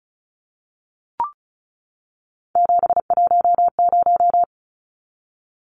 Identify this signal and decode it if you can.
Morse code, heard as a keyed tone, about 34 wpm, 700 Hz. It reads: 710
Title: Numbers - 3-Digit 35wpm